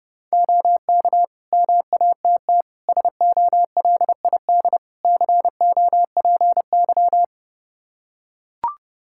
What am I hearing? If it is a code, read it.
Morse code, heard as a keyed tone, about 32 wpm, 700 Hz. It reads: OK MATT SOLID COPY
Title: Practice QSO - Exchange 3 - More Info - 30wpm